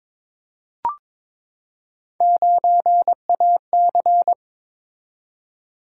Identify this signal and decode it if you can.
Morse code, heard as a keyed tone, about 22 wpm, 700 Hz. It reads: 9AC
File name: practice / 22wpm / Three Letter-Number Combinations - 22wpm